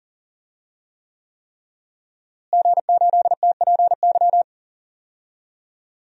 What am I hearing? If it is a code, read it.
Morse code, heard as a keyed tone, about 40 wpm, 700 Hz. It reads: G8TPY
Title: DXCC-100 Call Signs - 5 Characters - Repeated 3x - 40wpm